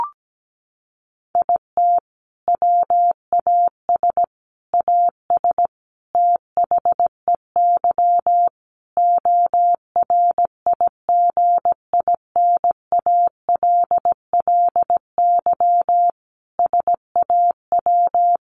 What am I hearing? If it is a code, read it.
Morse code, heard as a keyed tone, about 17 wpm, 700 Hz. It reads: IT WAS AS THEY ORIGINALLY SAW
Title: Sentences from Top 1500 Words 17wpm